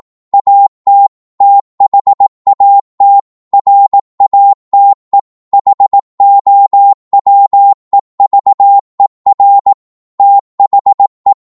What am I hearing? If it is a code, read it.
Morse code, heard as a keyed tone, about 18 wpm, 800 Hz. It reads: AT THAT RATE HOWEVER THE